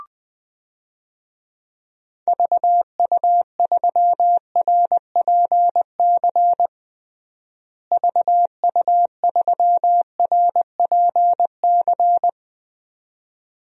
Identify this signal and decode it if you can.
Morse code, heard as a keyed tone, about 21 wpm, 700 Hz. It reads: VU3RPC VU3RPC
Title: DXCC-100 Call Signs - 6 Characters - Repeated 2x - 20wpm